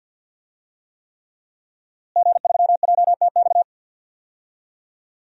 Morse code, heard as a keyed tone, about 49 wpm, 700 Hz. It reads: G3JTX
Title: DXCC-100 Call Signs - 5 Characters - Repeated 3x - 50wpm